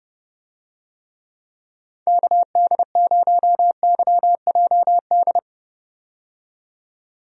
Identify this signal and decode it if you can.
Morse code, heard as a keyed tone, about 32 wpm, 700 Hz. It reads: KD0YJD
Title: USA Call Signs - 6 Characters - 30wpm